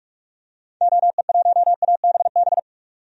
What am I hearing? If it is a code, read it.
Morse code, heard as a keyed tone, about 45 wpm, 700 Hz. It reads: OE1ABB
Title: DXCC-100 Call Signs - 6 Characters - 45wpm